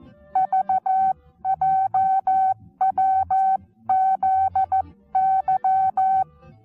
Morse code, heard as a keyed tone, about 14 wpm, 760 Hz. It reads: VJWZY
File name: Onlinelehrgang / morsen / DJ4UF Morsen19